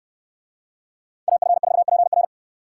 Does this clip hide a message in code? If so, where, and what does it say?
Yes: Morse code from 1.3 to 2.2 s, heard as a keyed tone, about 68 wpm, 700 Hz. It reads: AF4PR